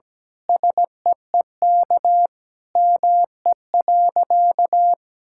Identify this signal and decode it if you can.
Morse code, heard as a keyed tone, about 17 wpm, 700 Hz. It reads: SEEK ME.
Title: Sentences from Top 1600 Words 17wpm